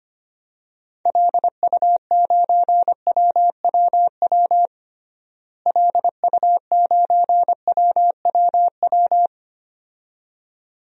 Morse code, heard as a keyed tone, about 25 wpm, 700 Hz. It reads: LU9WWW LU9WWW
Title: DXCC-100 Call Signs - 6 Characters - Repeated 2x - 25wpm